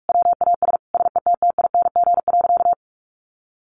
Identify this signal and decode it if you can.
Morse code, heard as a keyed tone, about 44 wpm, 700 Hz. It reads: WAS SETTING.
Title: Sentences from Top 1300 Words 45wpm